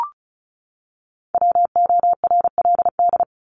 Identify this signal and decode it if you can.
Morse code, heard as a keyed tone, about 34 wpm, 700 Hz. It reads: WORLD
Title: Top 200 Words 35wpm